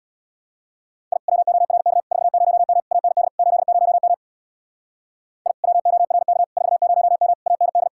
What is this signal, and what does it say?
Morse code, heard as a keyed tone, about 60 wpm, 700 Hz. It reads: I WORK 40M AND 80M I WORK 40M AND
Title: Speed-Racing Common QSO Phrases - 50wpm